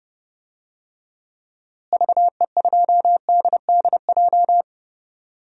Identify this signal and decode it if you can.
Morse code, heard as a keyed tone, about 31 wpm, 700 Hz. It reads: VE2DDJ
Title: DXCC-100 Call Signs - 6 Characters - Repeated 3x - 30wpm